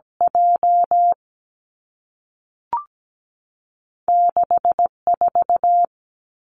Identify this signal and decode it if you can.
Morse code, heard as a keyed tone, about 17 wpm, 700 Hz. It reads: J 6 4
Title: Two Challenging Letter-Number Combinations 17wpm